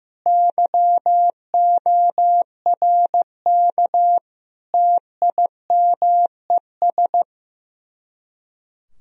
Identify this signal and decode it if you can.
Morse code, heard as a keyed tone, about 15 wpm, 700 Hz. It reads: YORK TIMES
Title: Speed-Racing - Sets of 5 Words 15wpm